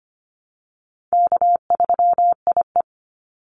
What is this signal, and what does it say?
Morse code, heard as a keyed tone, about 25 wpm, 700 Hz. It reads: K3IE
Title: Call Signs Repeated 3x - 25wpm